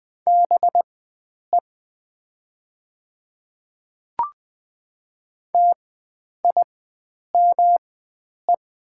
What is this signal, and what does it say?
Morse code, heard as a keyed tone, about 20 wpm, spaced as for 9 wpm, 700 Hz. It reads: BE TIME
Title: 4x Character Spacing - Top 300 Words - 20wpm